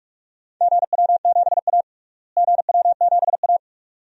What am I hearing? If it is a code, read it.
Morse code, heard as a keyed tone, about 42 wpm, 700 Hz. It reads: GW7A GW7A
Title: DXCC-100 Call Signs - 4 Characters - Repeated 2x - 45wpm